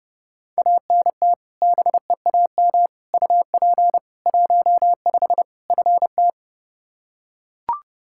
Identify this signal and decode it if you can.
Morse code, heard as a keyed tone, about 30 wpm, 700 Hz. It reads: ANT BEAM UP 15 FT
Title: Practice QSO - Exchange 2 - Info - 30wpm